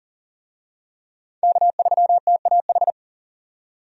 Morse code, heard as a keyed tone, about 40 wpm, 700 Hz. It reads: K3TAH
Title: USA Call Signs - 5 Characters - 40wpm